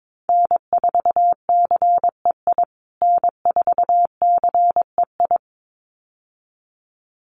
Morse code, heard as a keyed tone, about 22 wpm, 700 Hz. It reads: N4CEI N4CEI
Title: USA Call Signs Repeated 2x 22wpm